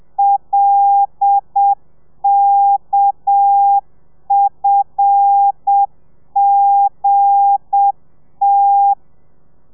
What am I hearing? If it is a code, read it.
Morse code, heard as a keyed tone, about 7 wpm, 800 Hz. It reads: LKFGT
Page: Lezione 1: audio a 30 CPM